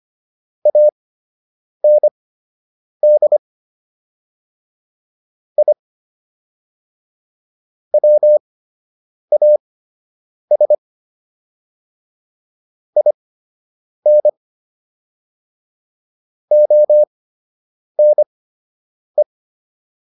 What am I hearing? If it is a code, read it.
Morse code, heard as a keyed tone, about 25 wpm, spaced as for 8 wpm, 600 Hz. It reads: AND I WAS IN ONE